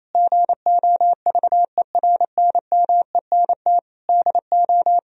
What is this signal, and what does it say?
Morse code, heard as a keyed tone, about 28 wpm, 700 Hz. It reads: GOVERNMENT DO
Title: Sentences from Top 1000 Words - Review Part 3 - 28wpm